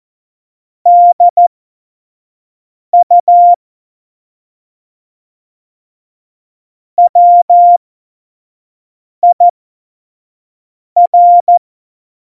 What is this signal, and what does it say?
Morse code, heard as a keyed tone, about 14 wpm, spaced as for 5 wpm, 700 Hz. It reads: DU WIR
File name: Morsekurs / Witze / Witz 19. Januar 2021, 25 F70